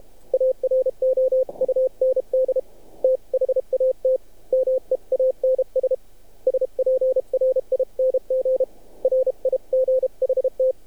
Morse code, heard as a keyed tone, about 30 wpm, 520 Hz. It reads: AROUND THAT MEANS SPRING RIGHT
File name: archived / CRSnet